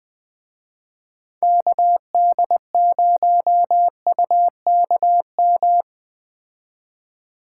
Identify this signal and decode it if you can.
Morse code, heard as a keyed tone, about 20 wpm, 700 Hz. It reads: KD0UKM